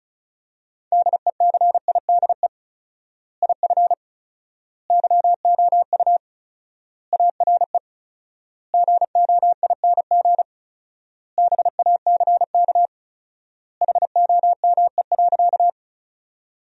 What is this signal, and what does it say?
Morse code, heard as a keyed tone, about 34 wpm, 700 Hz. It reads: DECIDE IF YOU ARE GOING BACK HOME.
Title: 4x Word-Spacing - Sentences from Top 500 Words - 35wpm